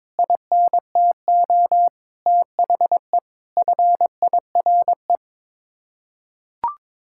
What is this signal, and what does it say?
Morse code, heard as a keyed tone, about 22 wpm, 700 Hz. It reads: INTO THE FIRE
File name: Sets of 9 Words - 22wpm